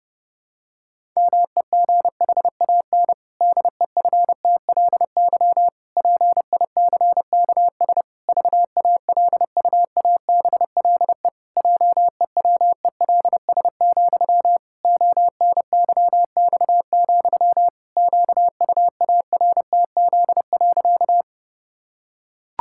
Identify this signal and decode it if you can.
Morse code, heard as a keyed tone, about 30 wpm, 700 Hz. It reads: MEGHAN DEFTLY PICKS VALUABLE JEWELS, ONYX, QUARTZ.
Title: Pangram Challenge v2 - 30wpm